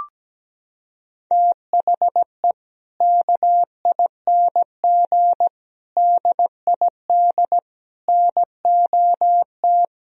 Morse code, heard as a keyed tone, about 17 wpm, 700 Hz. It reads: THE KING DID NOT
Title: Sentences from Top 1000 Words - Review Part 2 - 17wpm